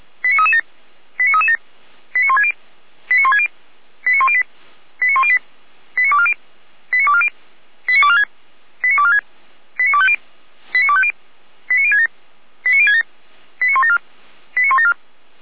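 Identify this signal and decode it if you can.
Five-tone selective call (ZVEI), heard as quick runs of five tones, tones of about 70 ms. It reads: 89337 89337 89160 89160 89108 89108 89240 89240 89246 89246 89250 89250 89067 89067 89174 89174
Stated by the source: ZVEI five-tone call 89075 received by SDR.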